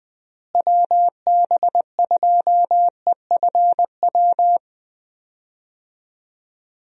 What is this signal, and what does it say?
Morse code, heard as a keyed tone, about 20 wpm, 700 Hz. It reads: WB2EFW
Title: USA Call Signs - 6 Characters - Repeated 3x - 20wpm